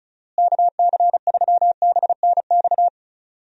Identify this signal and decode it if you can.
Morse code, heard as a keyed tone, about 36 wpm, 700 Hz. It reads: KC3BNX
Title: USA Call Signs - 6 Characters - Repeated 2x - 35wpm